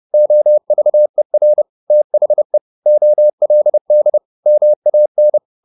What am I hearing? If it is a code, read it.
Morse code, heard as a keyed tone, about 30 wpm, 600 Hz. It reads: OVER THE OLD MAN
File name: the old man and the sea / the old man and the sea 1 30wpm